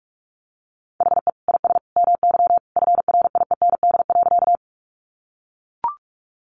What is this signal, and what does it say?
Morse code, heard as a keyed tone, about 44 wpm, 700 Hz. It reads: HE IS MY FRIEND.
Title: Sentences from Top 300 Words 45wpm